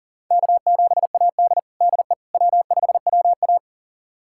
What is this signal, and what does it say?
Morse code, heard as a keyed tone, about 40 wpm, 700 Hz. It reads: K7AD DE W5WA
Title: USA Callsign DE Callsign - 4 Characters - 40wpm